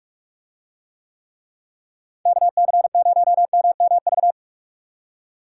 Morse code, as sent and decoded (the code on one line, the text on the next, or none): -.- -.-. ----- -- -- ...-
KC0MMV